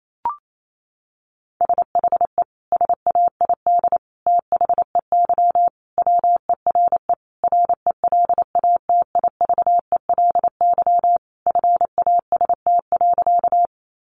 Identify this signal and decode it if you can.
Morse code, heard as a keyed tone, about 28 wpm, 700 Hz. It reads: SHE SAID THEY WERE RELATIVELY FAST.
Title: Sentences from Top 1100 Words 28wpm